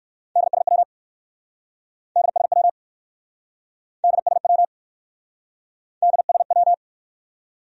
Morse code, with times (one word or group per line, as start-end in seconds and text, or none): 0.4-0.8 DSW
2.2-2.7 DSW
4.0-4.6 DSW
6.0-6.7 DSW